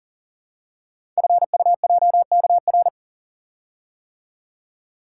Morse code, heard as a keyed tone, about 42 wpm, 700 Hz. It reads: FUJKR
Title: Code Groups - 5 Letters - 40wpm